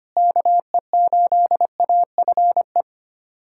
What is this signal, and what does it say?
Morse code, heard as a keyed tone, about 25 wpm, 700 Hz. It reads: KE8AFE
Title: USA Callsign DE Callsign - 6 Characters - 25wpm